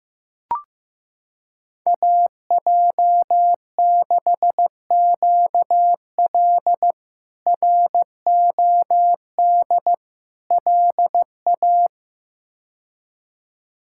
Morse code, heard as a keyed tone, about 15 wpm, 700 Hz. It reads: AJ6QL ROD LA
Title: K1USN SST QSO Practice - 15wpm